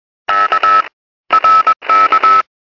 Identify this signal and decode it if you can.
Morse code, heard as a keyed tone, about 21 wpm, 1300 Hz, with noise. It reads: K RK
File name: ham stuff / cw squelched